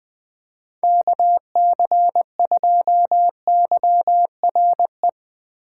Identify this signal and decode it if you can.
Morse code, heard as a keyed tone, about 20 wpm, 700 Hz. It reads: KC2YRE
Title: USA Call Signs - 6 Characters - 3x Character Spacing - 20wpm